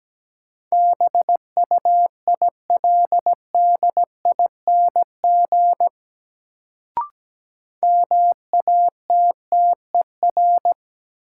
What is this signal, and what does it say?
Morse code, heard as a keyed tone, about 17 wpm, 700 Hz. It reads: BUILDING MATTER
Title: Sentences from Top 700 Words - New Word Drill - 17wpm